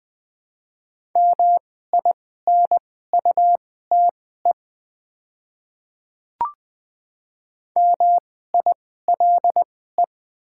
Morse code, heard as a keyed tone, about 20 wpm, spaced as for 15 wpm, 700 Hz. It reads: MINUTE MILE
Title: 2x Character Spacing - Top 500 Words - 20wpm